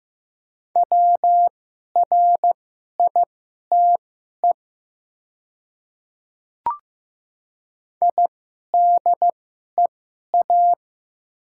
Morse code, heard as a keyed tone, about 15 wpm, spaced as for 11 wpm, 700 Hz. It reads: WRITE IDEA
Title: 2x Character Spacing - Top 500 Words - 15wpm